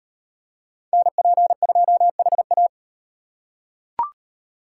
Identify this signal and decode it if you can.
Morse code, heard as a keyed tone, about 38 wpm, 700 Hz. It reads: NP2HA